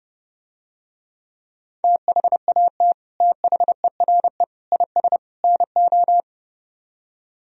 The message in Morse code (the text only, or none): THAT THERE IS NO